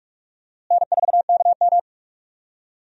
Morse code, heard as a keyed tone, about 46 wpm, 700 Hz. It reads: N4KM
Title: USA Call Signs - 4 Characters - Repeated 2x - 45wpm